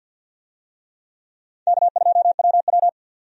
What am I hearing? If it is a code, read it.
Morse code, heard as a keyed tone, about 51 wpm, 700 Hz. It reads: K2WW
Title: USA Call Signs - 4 Characters - 50wpm